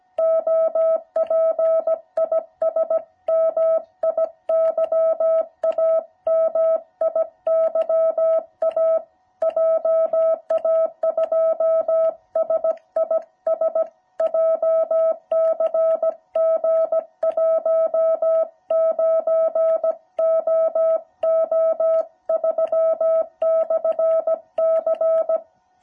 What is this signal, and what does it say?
Morse code, heard as a keyed tone, about 17 wpm, 650 Hz. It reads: OP IS MIYA MIYA JA2SIS JCG 19OO3/C